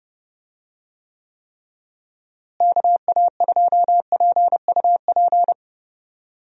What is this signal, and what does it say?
Morse code, heard as a keyed tone, about 30 wpm, 700 Hz. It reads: KA2PUP
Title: USA Call Signs - 6 Characters - Repeated 3x - 30wpm